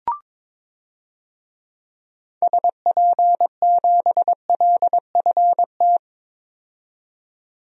Morse code, heard as a keyed tone, about 22 wpm, 700 Hz. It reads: SP7LFT
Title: DXCC-100 Call Signs - 6 Characters - 22wpm